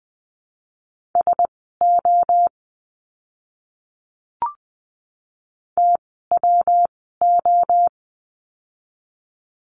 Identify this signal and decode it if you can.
Morse code, heard as a keyed tone, about 20 wpm, spaced as for 15 wpm, 700 Hz. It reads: SO TWO
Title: 230 Lesson 37f - Letter x - Words 20wpm